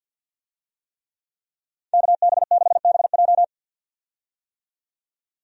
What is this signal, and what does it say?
Morse code, heard as a keyed tone, about 48 wpm, 700 Hz. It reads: KB6BJ
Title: USA Call Signs - 5 Characters - 50wpm